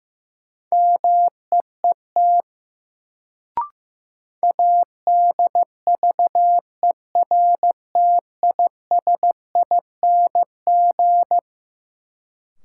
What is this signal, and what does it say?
Morse code, heard as a keyed tone, about 15 wpm, 700 Hz. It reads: MEET ADVERTISING
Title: Sentences from Top 1000 Words - Review Part 6 - New Word Drill - 15wpm